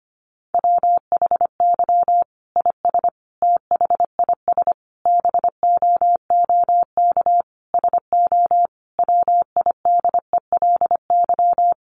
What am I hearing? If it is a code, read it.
Morse code, heard as a keyed tone, about 25 wpm, 700 Hz. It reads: WHY IS THIS BOOK SO WIDELY
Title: Sentences from Top 1200 Words 25wpm